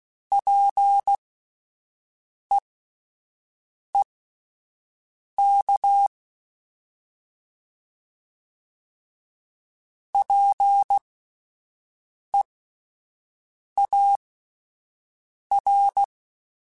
Morse code, heard as a keyed tone, about 16 wpm, spaced as for 5 wpm, 800 Hz. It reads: PEEK PEAR